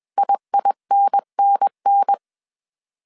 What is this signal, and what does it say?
Morse code, heard as a keyed tone, about 21 wpm, 800 Hz. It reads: IINNN
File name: CW / DET / 1 3 1500C S B 20wpm 1754